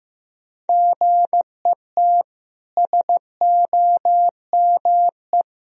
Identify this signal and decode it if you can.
Morse code, heard as a keyed tone, about 15 wpm, 700 Hz. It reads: GET SOME
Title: Sentences from Top 1000 Words - Review Part 4 - 15wpm